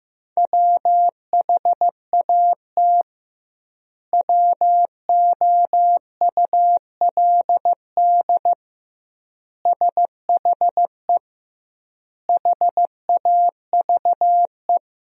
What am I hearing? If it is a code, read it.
Morse code, heard as a keyed tone, about 15 wpm, 700 Hz. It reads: WHAT WOULD SHE HAVE
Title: 2x Word-Spacing - Sentences from Top 100 Words - 15wpm